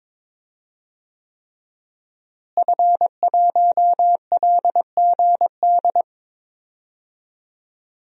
Morse code, heard as a keyed tone, about 22 wpm, 700 Hz. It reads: F1LGD
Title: DXCC-100 Call Signs - 5 Characters - Repeated 3x - 22wpm